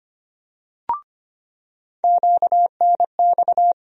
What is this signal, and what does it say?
Morse code, heard as a keyed tone, about 25 wpm, 700 Hz. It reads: QNX
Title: USA Call Sign Postfixes - 3 Letters 25wpm